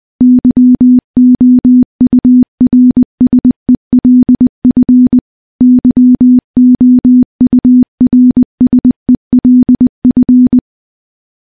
Morse code, heard as a keyed tone, about 20 wpm, 250 Hz. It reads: YOURSELF YOURSELF
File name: morse-training-20WPM